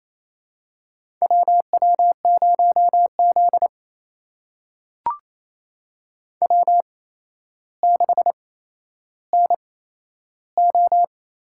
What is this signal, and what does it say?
Morse code, heard as a keyed tone, about 28 wpm, spaced as for 12 wpm, 700 Hz. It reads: WW0Z W6NO
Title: USA Call Signs - 4 Characters - 8x Character Spacing - 28wpm